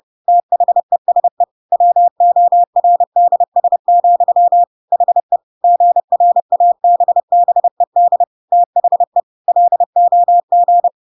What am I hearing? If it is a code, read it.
Morse code, heard as a keyed tone, about 30 wpm, 700 Hz. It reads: THESE WORDS, HE GRABBED THE LOG
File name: Pinocchio / Pinocchio ch1 30 30 ik1hge